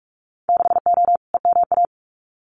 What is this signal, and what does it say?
Morse code, heard as a keyed tone, about 46 wpm, 700 Hz. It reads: 6O EMA
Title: Winter Field Day Exchanges Encore 45wpm